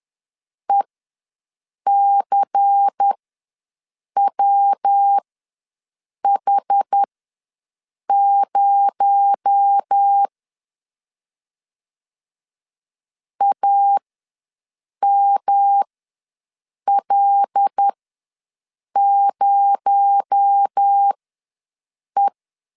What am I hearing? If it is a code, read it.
Morse code, heard as a keyed tone, about 10 wpm, spaced as for 6 wpm, 800 Hz. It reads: ECWH0 AML0E
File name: LEZIONI / 7 GF9Z / 7 G ws 10wpm 3843